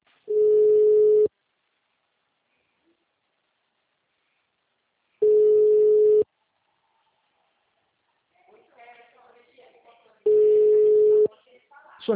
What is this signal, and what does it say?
Ringback tone (Europe, 425 Hz), with speech.